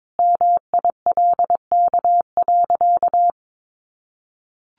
Morse code, heard as a keyed tone, about 22 wpm, 700 Hz. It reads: MILK.
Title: Additional Easy Phrases 22wpm